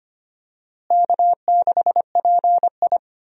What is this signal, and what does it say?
Morse code, heard as a keyed tone, about 25 wpm, 700 Hz. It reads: K6PI
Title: USA Call Signs - 4 Characters - Repeated 2x - 25wpm